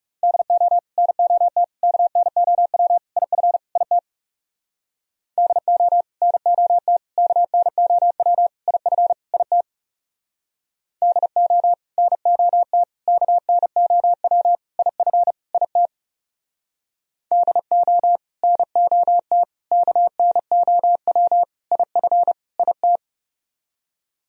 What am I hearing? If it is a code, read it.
Morse code, heard as a keyed tone, about 40 wpm, 700 Hz. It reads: DO NOT KNOW IF IT DO NOT KNOW IF IT DO NOT KNOW IF IT DO NOT KNOW IF IT